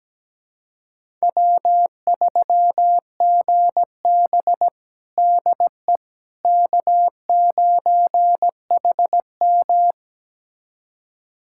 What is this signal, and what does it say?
Morse code, heard as a keyed tone, about 17 wpm, 700 Hz. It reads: W3GB DE K9HM
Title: USA Callsign DE Callsign - 4 Characters - 17wpm